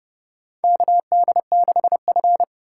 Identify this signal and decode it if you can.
Morse code, heard as a keyed tone, about 31 wpm, 700 Hz. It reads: KD6F
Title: USA Call Signs - 4 Characters - 2x Character Spacing - 30wpm